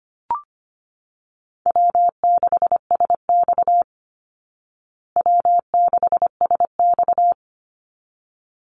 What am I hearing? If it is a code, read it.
Morse code, heard as a keyed tone, about 25 wpm, 700 Hz. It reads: W6SX W6SX
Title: Call Signs Repeated 3x - 25wpm